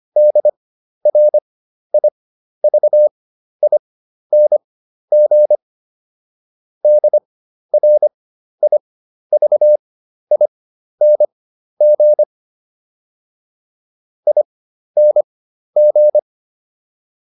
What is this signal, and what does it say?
Morse code, heard as a keyed tone, about 25 wpm, spaced as for 12 wpm, 600 Hz. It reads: DRIVING DRIVING ING